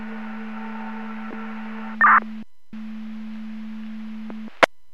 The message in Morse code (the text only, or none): none